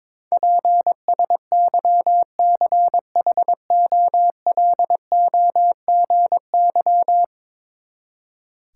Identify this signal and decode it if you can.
Morse code, heard as a keyed tone, about 21 wpm, 700 Hz. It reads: PSYCHOLOGY
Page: morse-code-ninja-practice-22wpm